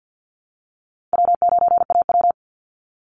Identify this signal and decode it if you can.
Morse code, heard as a keyed tone, about 49 wpm, 700 Hz. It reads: W9AW
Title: Call Signs Repeated 3x - 50wpm